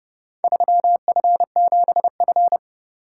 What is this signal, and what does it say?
Morse code, heard as a keyed tone, about 27 wpm, 700 Hz. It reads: 3F7F